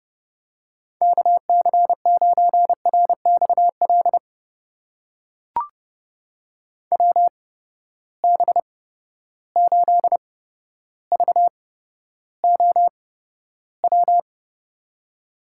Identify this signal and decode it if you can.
Morse code, heard as a keyed tone, about 30 wpm, spaced as for 13 wpm, 700 Hz. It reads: KC9RXL WB8VOW
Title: USA Call Signs - 6 Characters - 8x Character Spacing - 30wpm